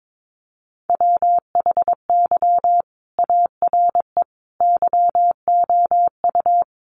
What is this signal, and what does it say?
Morse code, heard as a keyed tone, about 22 wpm, 700 Hz. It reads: WHY ARE YOU
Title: Sentences from Top 200 Words 22wpm